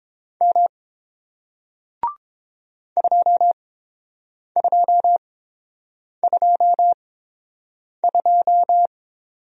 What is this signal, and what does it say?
Morse code, heard as a keyed tone, about 30 wpm, spaced as for 7 wpm, 700 Hz. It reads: M 2222